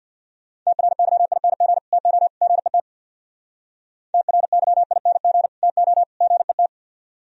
Morse code, heard as a keyed tone, about 63 wpm, 700 Hz. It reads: TRYING TO GET TRYING TO GET